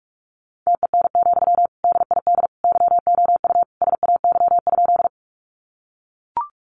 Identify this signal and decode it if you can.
Morse code, heard as a keyed tone, about 44 wpm, 700 Hz. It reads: TEN, DID YOU SAY?
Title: Sentences from Top 400 Words 45wpm